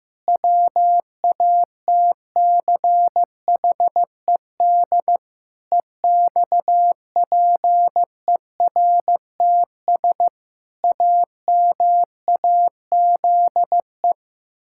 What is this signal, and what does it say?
Morse code, heard as a keyed tone, about 16 wpm, 700 Hz. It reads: WATCHED EXPERTS AMAZE
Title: Pangram Challenge v2 - 15wpm